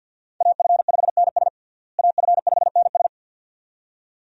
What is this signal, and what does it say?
Morse code, heard as a keyed tone, about 47 wpm, 700 Hz. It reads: AF5NS AF5NS
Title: USA Call Signs - 5 Characters - Repeated 2x - 50wpm